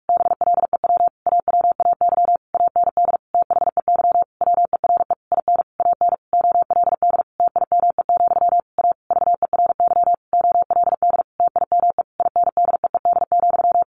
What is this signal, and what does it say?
Morse code, heard as a keyed tone, about 44 wpm, 700 Hz. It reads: BLEW AWAY AND THEY WERE IN AN OLD TIME, A VERY OLD TIME INDEED,
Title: A Sound Of Thunder 45wpm